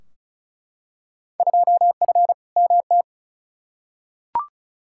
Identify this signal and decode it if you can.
Morse code, heard as a keyed tone, about 34 wpm, 700 Hz. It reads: 2F MT
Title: Field Day Exchanges Encore - 35wpm